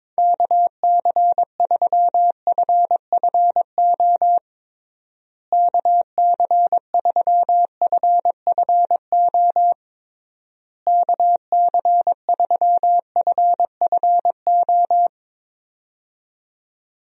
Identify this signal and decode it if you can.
Morse code, heard as a keyed tone, about 22 wpm, 700 Hz. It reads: KC3FFO KC3FFO KC3FFO
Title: USA Call Signs - 6 Characters - Repeated 3x - 22wpm